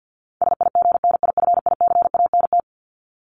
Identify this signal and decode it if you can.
Morse code, heard as a keyed tone, about 50 wpm, 700 Hz. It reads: SIGNIFICANT
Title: LY Suffix - Top 100 Words with Base Words 50wpm